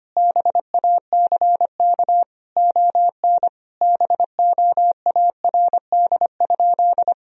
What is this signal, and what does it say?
Morse code, heard as a keyed tone, about 25 wpm, 700 Hz. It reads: BACK ON BOARD?
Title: Sentences from Top 1000 Words - Review Part 4 - 25wpm